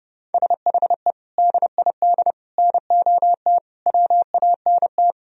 Morse code, heard as a keyed tone, about 37 wpm, 700 Hz. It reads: SHE DID NOT WANT
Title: Speed-Racing - Sets of 5 Words 30wpm